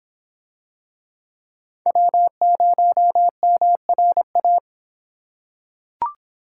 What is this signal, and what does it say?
Morse code, heard as a keyed tone, about 24 wpm, 700 Hz. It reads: W0MRA